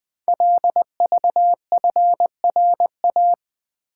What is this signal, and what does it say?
Morse code, heard as a keyed tone, about 20 wpm, 700 Hz. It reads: LVFRA
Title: Code Groups - 5 Letters - 20wpm